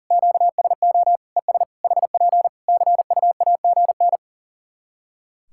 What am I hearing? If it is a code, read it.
Morse code, heard as a keyed tone, about 40 wpm, 700 Hz. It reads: QSO ES HP CUAGN
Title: K1USN SST QSO Practice 40wpm